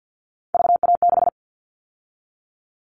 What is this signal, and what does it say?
Morse code, heard as a keyed tone, about 50 wpm, 700 Hz. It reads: VA6
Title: Callsign Prefixes for Top 50 DXCC - 50wpm